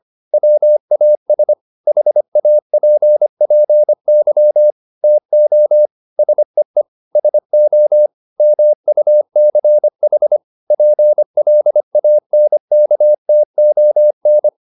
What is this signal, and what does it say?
Morse code, heard as a keyed tone, about 25 wpm, 600 Hz. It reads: WAS HAPPY TO SEE SO MUCH PLANKTON
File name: the old man and the sea part 1 25wpm